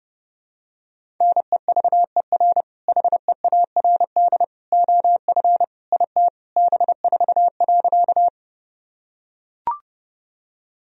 Morse code, heard as a keyed tone, about 29 wpm, 700 Hz. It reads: NEVER HEARD OF IT B4.
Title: Sentences from Top 400 Words and Abbreviations 30wpm